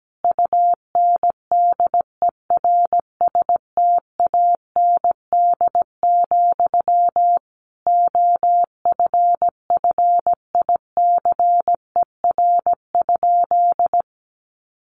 Morse code, heard as a keyed tone, about 17 wpm, 700 Hz. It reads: UNDERSTAND, OFFICER?
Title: Sentences from Top 1200 Words 17wpm